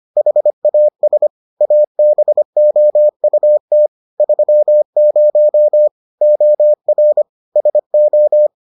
Morse code, heard as a keyed tone, about 25 wpm, 600 Hz. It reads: HAS ABOUT 30 OR SO